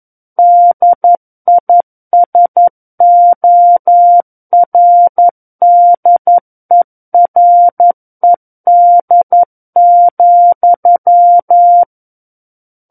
Morse code, heard as a keyed tone, about 11 wpm, 700 Hz. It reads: DISORDERED,
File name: 11 WPM